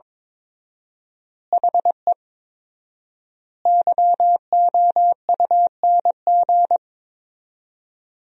Morse code, heard as a keyed tone, about 22 wpm, 700 Hz. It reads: HE YOUNG
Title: 4x Word-Spacing - Sentences from Top 400 Words - 22wpm